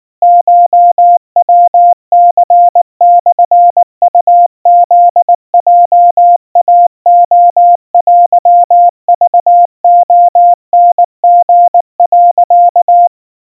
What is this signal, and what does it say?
Morse code (Japanese Wabun code), heard as a keyed tone, about 19 wpm, 700 Hz. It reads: コヤニモウフヲイレテクレタリ、
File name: kawa / Morse sample / kamyu 70 cpm 700 Hz Crispy